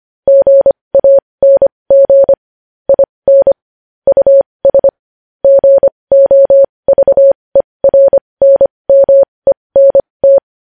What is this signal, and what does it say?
Morse code, heard as a keyed tone, about 25 wpm, spaced as for 20 wpm, 550 Hz. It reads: GANG IN US GOVERNMENT
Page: News Headlines in Morse Code at 20 WPM